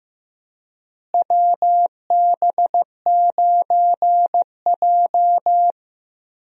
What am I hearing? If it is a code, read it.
Morse code, heard as a keyed tone, about 15 wpm, 700 Hz. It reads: WB9J
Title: USA Call Signs - 4 Characters - Repeated 3x - 15wpm